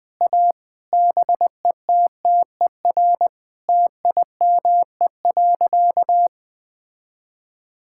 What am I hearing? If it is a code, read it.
Morse code, heard as a keyed tone, about 21 wpm, 700 Hz. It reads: A BETTER TIME.
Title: Sentences with 8-Word Phrases - Spoken Beginning - 20wpm